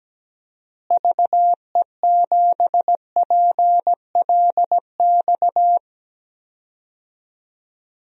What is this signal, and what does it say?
Morse code, heard as a keyed tone, about 17 wpm, 700 Hz. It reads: VE7PLX
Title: DXCC-100 Call Signs - 6 Characters - Repeated 2x - 17wpm